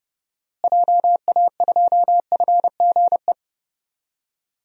Morse code, heard as a keyed tone, about 30 wpm, 700 Hz. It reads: JA2FGE
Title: DXCC-100 Call Signs - 6 Characters - Repeated 2x - 30wpm